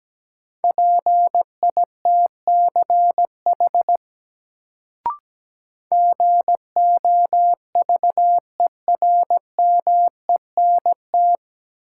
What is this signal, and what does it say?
Morse code, heard as a keyed tone, about 17 wpm, 700 Hz. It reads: PITCH GOVERMENT
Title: Sentences from Top 1300 Words - New Word Drill - 17wpm